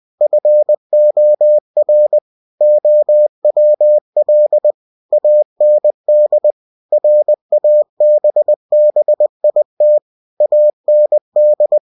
Morse code, heard as a keyed tone, about 20 wpm, 600 Hz. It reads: FOR OWL AND RABBIT AND
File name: cw-books / winnie-the-pooh / winnie-20wpm-page-0081